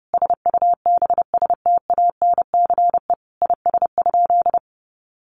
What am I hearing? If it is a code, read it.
Morse code, heard as a keyed tone, about 30 wpm, 700 Hz. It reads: SUBSTANCE IS?
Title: Sentences from Top 1500 Words 30wpm